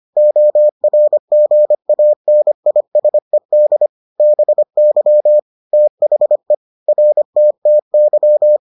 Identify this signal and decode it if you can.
Morse code, heard as a keyed tone, about 25 wpm, 600 Hz. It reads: ORGANISED BY THE RTTY